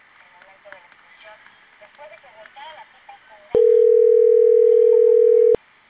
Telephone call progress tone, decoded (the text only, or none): ringback tone